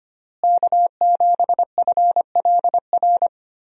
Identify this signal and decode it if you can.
Morse code, heard as a keyed tone, about 25 wpm, 700 Hz. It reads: K7FLR
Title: USA Call Signs - 5 Characters - Repeated 3x - 25wpm